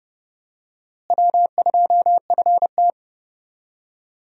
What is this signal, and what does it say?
Morse code, heard as a keyed tone, about 29 wpm, 700 Hz. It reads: W2FT
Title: USA Call Signs - 4 Characters - Repeated 3x - 30wpm